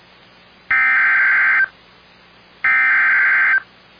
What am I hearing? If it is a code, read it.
Emergency Alert System (SAME) tones (header bursts), heard as harsh buzzing bursts. It is heard from 0.7 to 3.5 s.